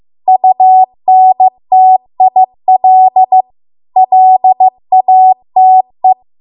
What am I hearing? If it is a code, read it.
Morse code, heard as a keyed tone, about 15 wpm, 750 Hz. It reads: UNTIL LATE